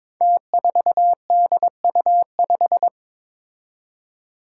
Morse code, heard as a keyed tone, about 22 wpm, 700 Hz. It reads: T4DU5
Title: Five Challenging Letter-Number Combinations 22wpm